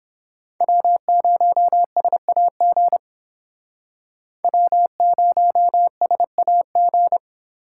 Morse code, heard as a keyed tone, about 31 wpm, 700 Hz. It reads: W0SAG W0SAG